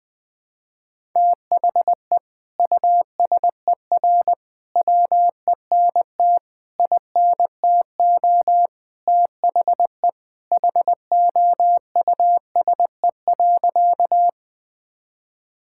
Morse code, heard as a keyed tone, about 20 wpm, 700 Hz. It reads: THE USER WENT INTO THE HOUSE.
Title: Sentences from Top 1000 Words - Review Part 5 - 20wpm